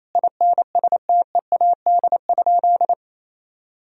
Morse code, heard as a keyed tone, about 28 wpm, 700 Hz. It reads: INSTEAD?
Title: Sentences from Top 1000 Words - Review Part 4 - 28wpm